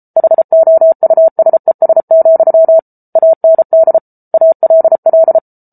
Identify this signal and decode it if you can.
Morse code, heard as a keyed tone, about 34 wpm, 650 Hz. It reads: HOUSES, AND ALL